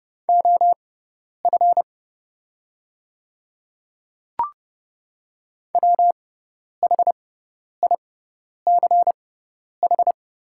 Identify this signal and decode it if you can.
Morse code, heard as a keyed tone, about 30 wpm, spaced as for 10 wpm, 690 Hz. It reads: OF WHICH
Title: 6x Character Spacing - Top 500 Words - 30wpm